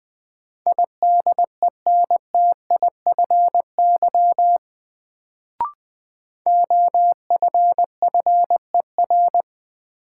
Morse code, heard as a keyed tone, about 20 wpm, 700 Hz. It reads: IDENTIFY OFFER
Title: Sentences from Top 900 Words - New Word Drill - 20wpm